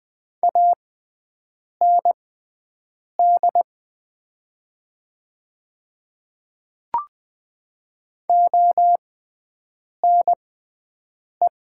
Morse code, heard as a keyed tone, about 20 wpm, spaced as for 7 wpm, 700 Hz. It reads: AND ONE
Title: 6x Character Spacing - Top 300 Words - 20wpm